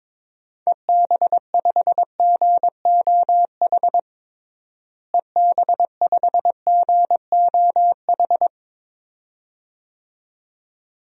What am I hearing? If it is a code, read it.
Morse code, heard as a keyed tone, about 22 wpm, 700 Hz. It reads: EB5GOH EB5GOH
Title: DXCC-100 Call Signs - 6 Characters - Repeated 2x - 22wpm